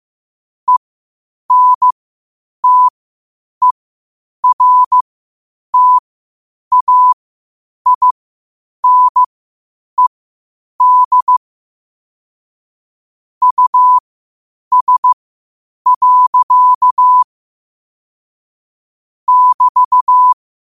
Morse code, heard as a keyed tone, about 15 wpm, spaced as for 8 wpm, 1000 Hz. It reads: ENTERTAINED US. <BT>
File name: qotdfiles / 8wpm